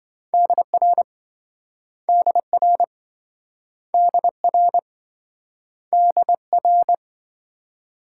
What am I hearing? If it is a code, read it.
Morse code, heard as a keyed tone, about 26 wpm, 700 Hz. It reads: DR DR DR DR